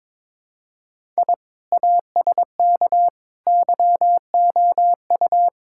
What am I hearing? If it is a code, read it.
Morse code, heard as a keyed tone, about 22 wpm, 700 Hz. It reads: I ASK YOU
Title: Sentences from Top 1000 Words - Review Part 5 - 22wpm